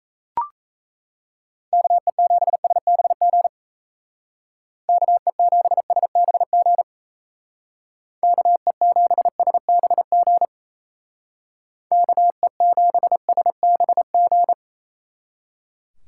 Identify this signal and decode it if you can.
Morse code, heard as a keyed tone, about 34 wpm, 700 Hz. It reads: KE7SBG KE7SBG KE7SBG KE7SBG
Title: Speed Racing - USA 2x3 Call Signs 28wpm